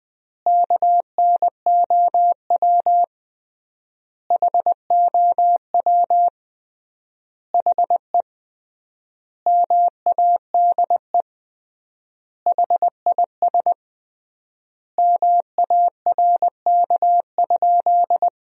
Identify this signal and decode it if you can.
Morse code, heard as a keyed tone, about 21 wpm, 700 Hz. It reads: KNOW HOW HE MADE HIS MARK?
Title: 3x Word-Spacing - Sentences from Top 300 Words - 20wpm